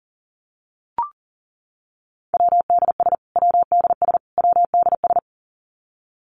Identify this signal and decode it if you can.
Morse code, heard as a keyed tone, about 40 wpm, 700 Hz. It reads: WDS WDS WDS
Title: Abbreviations Repeated 3x - 40wpm